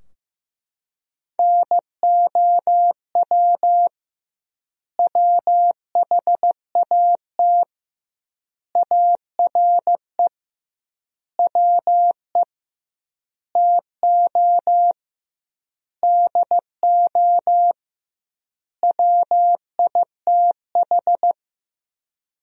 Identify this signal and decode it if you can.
Morse code, heard as a keyed tone, about 15 wpm, 700 Hz. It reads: NOW WHAT ARE WE TO DO WITH
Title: 2x Word-Spacing - Sentences from Top 100 Words - 15wpm